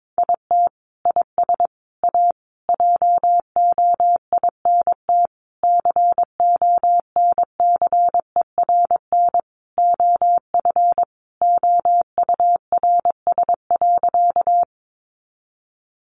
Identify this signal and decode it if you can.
Morse code, heard as a keyed tone, about 22 wpm, 700 Hz. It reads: IT IS A JOINT CONCERN OF OURS.
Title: Sentences from Top 1300 Words 22wpm